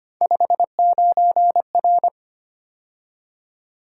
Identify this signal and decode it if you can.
Morse code, heard as a keyed tone, about 25 wpm, 700 Hz. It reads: 59R
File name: Three Letter-Number Combinations - 25wpm